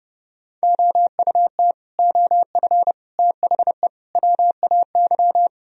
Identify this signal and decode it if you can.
Morse code, heard as a keyed tone, about 30 wpm, 700 Hz. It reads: OUT OF THE WAY
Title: Speed-Racing - Sets of 4 Words 25wpm